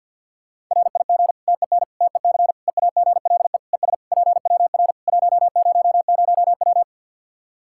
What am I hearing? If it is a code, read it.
Morse code, heard as a keyed tone, about 53 wpm, 700 Hz. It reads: RIG TEN TEC EAGLE ES PWR 100W
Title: Practice QSO - Exchange 2 - Info - 50wpm